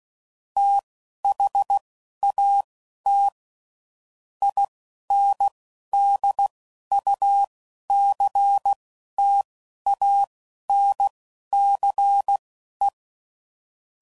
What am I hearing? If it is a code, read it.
Morse code, heard as a keyed tone, about 16 wpm, spaced as for 11 wpm, 800 Hz. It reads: THAT INDUCTANCE
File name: CW Code Groups @10WPM / Technical Subjects / Reactance